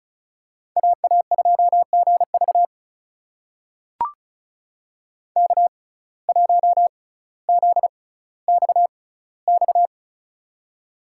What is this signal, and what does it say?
Morse code, heard as a keyed tone, about 36 wpm, spaced as for 18 wpm, 700 Hz. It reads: AA2GV K1ZXX
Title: USA Call Signs - 5 Characters - 6x Character Spacing - 35wpm